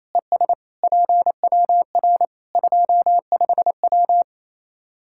Morse code, heard as a keyed tone, about 27 wpm, 700 Hz. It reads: ES PWR 25W
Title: K1USN SST QSO Practice 28wpm